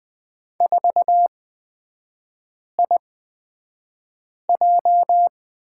Morse code, heard as a keyed tone, about 20 wpm, spaced as for 5 wpm, 700 Hz. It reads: 4IJ
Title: 224 Lesson 36e - BT - Three Characters 20wpm